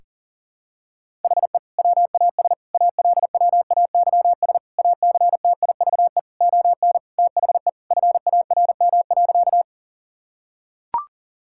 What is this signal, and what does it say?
Morse code, heard as a keyed tone, about 40 wpm, 700 Hz. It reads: HE WAS ALWAYS ACTIVE ON THE FARM.
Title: Sentences from Top 1000 Words - Review Part 6 - 40wpm